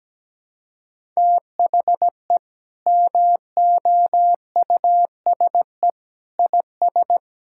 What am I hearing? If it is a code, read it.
Morse code, heard as a keyed tone, about 17 wpm, 700 Hz. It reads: THE MOUSE IS
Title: Sentences from Top 1000 Words - Review Part 5 - 17wpm